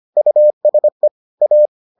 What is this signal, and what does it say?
Morse code, heard as a keyed tone, about 25 wpm, 600 Hz. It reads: USE A